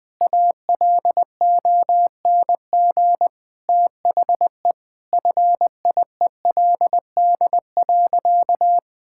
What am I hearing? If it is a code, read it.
Morse code, heard as a keyed tone, about 21 wpm, 700 Hz. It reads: ALONG THE FIELD.
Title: Sentences from Top 1000 Words - Review Part 5 - 20wpm